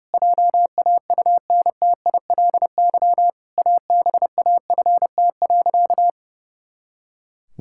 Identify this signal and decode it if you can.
Morse code, heard as a keyed tone, about 32 wpm, 700 Hz. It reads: JAUNTILY ABAFT.
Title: Pangram Challenge v2 - 30wpm